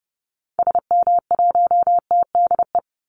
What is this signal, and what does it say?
Morse code, heard as a keyed tone, about 30 wpm, 700 Hz. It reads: SM1TDE
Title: Call Signs Repeated 2x - 30wpm